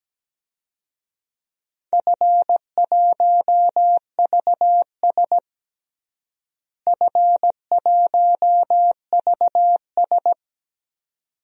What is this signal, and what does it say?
Morse code, heard as a keyed tone, about 17 wpm, 700 Hz. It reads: F1VS F1VS
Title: DXCC-100 Call Signs - 4 Characters - Repeated 2x - 17wpm